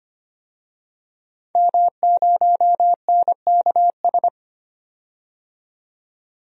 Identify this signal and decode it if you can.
Morse code, heard as a keyed tone, about 25 wpm, 700 Hz. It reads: M0NKS
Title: DXCC-100 Call Signs - 5 Characters - 25wpm